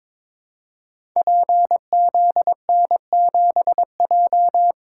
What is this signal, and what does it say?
Morse code, heard as a keyed tone, about 22 wpm, 700 Hz. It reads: PZN7J
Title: Code Groups - 5 Letter:Numbers - 22wpm